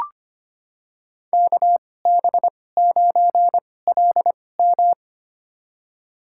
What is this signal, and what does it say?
Morse code, heard as a keyed tone, about 25 wpm, spaced as for 18 wpm, 700 Hz. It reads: KB9LM
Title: USA Call Signs - 2x Character Spacing - 25wpm